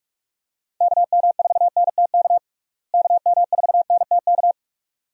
Morse code, heard as a keyed tone, about 45 wpm, 700 Hz. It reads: KM4NTK KM4NTK
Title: USA Call Signs - 6 Characters - Repeated 2x - 45wpm